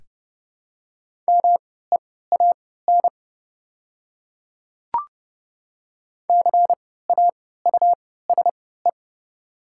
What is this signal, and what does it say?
Morse code, heard as a keyed tone, about 30 wpm, spaced as for 17 wpm, 700 Hz. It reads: MEAN CAUSE
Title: 3x Character Spacing - Top 300 Words - 30wpm